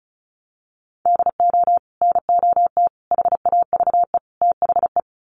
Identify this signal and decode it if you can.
Morse code, heard as a keyed tone, about 34 wpm, 700 Hz. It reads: DO NOT HAVE THE
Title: Sets of 4 Words 35wpm